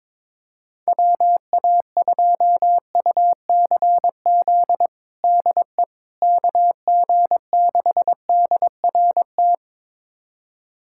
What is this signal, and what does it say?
Morse code, heard as a keyed tone, about 22 wpm, 700 Hz. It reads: WA2UCZ DE KG6DRT
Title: USA Callsign DE Callsign - 6 Characters - 22wpm